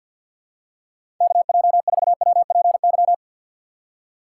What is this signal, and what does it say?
Morse code, heard as a keyed tone, about 52 wpm, 700 Hz. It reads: KJ4WPY
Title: USA Call Signs - 6 Characters - 4x Character Spacing - 50wpm